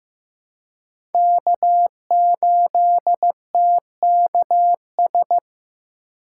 Morse code, heard as a keyed tone, about 15 wpm, 700 Hz. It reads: K8TKS
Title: USA Call Signs - 5 Characters - 4x Character Spacing - 15wpm